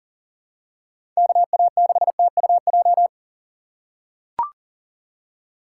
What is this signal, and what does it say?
Morse code, heard as a keyed tone, about 40 wpm, 700 Hz. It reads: KA6TUJ
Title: USA Call Signs - 6 Characters - 2x Character Spacing - 40wpm